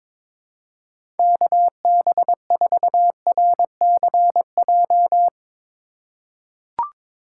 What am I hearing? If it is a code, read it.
Morse code, heard as a keyed tone, about 22 wpm, 700 Hz. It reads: KB4RCJ
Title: USA Call Signs - 6 Characters - 2x Character Spacing - 22wpm